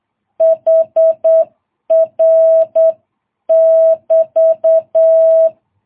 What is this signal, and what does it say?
Morse code, heard as a keyed tone, about 8 wpm, 660 Hz. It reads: HR <BT>
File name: jh8cbh